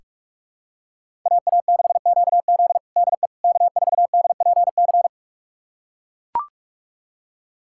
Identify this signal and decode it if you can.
Morse code, heard as a keyed tone, about 42 wpm, 700 Hz. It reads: AA6QZ DE K4DPC
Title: USA Callsign DE Callsign - 5 Characters - 45wpm